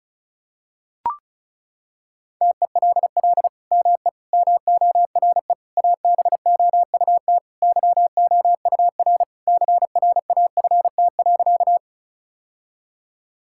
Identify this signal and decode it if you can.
Morse code, heard as a keyed tone, about 34 wpm, 700 Hz. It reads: TELL ME MORE ABOUT YOUR CRAFT.
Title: Sentences from Top 1000 Words - Review Part 6 - 35wpm